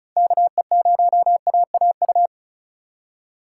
Morse code, heard as a keyed tone, about 34 wpm, 700 Hz. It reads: KE0AAU
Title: USA Call Signs - 4x Character Spacing - 35wpm